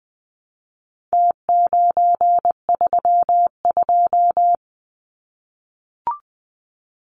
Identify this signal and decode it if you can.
Morse code, heard as a keyed tone, about 20 wpm, 700 Hz. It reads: T932
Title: Numbers - Contest Style - 4-Digit 20wpm